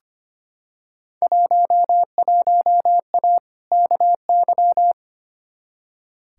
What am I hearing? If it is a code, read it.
Morse code, heard as a keyed tone, about 25 wpm, 700 Hz. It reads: 11A KY
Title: Field Day Exchanges Encore - 25wpm